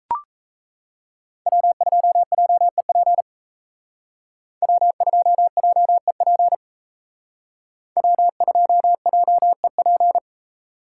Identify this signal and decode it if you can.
Morse code, heard as a keyed tone, about 32 wpm, 700 Hz. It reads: W2JEP W2JEP W2JEP
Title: Speed Racing - USA 1x3 Call Signs 28wpm